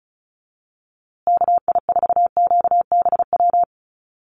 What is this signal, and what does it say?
Morse code, heard as a keyed tone, about 34 wpm, 700 Hz. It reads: KI4QBW
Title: USA Call Signs Repeated 2x 35wpm